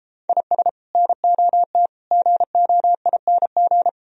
Morse code, heard as a keyed tone, about 31 wpm, 700 Hz. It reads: IS NOT GOING